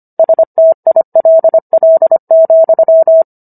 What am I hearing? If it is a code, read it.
Morse code, heard as a keyed tone, about 25 wpm, 650 Hz. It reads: STILL,